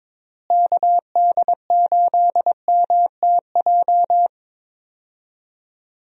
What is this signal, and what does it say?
Morse code, heard as a keyed tone, about 21 wpm, 700 Hz. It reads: KD8MTJ
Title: USA Call Signs - 6 Characters - Repeated 3x - 22wpm